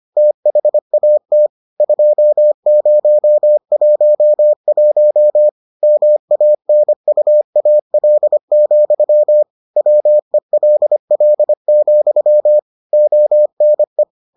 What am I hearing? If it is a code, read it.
Morse code, heard as a keyed tone, about 25 wpm, 600 Hz. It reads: THAT 2011 MANUAL, WELL, ONE